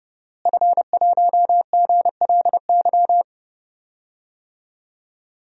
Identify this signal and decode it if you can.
Morse code, heard as a keyed tone, about 31 wpm, 700 Hz. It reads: F1GLY
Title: DXCC-100 Call Signs - 5 Characters - Repeated 2x - 30wpm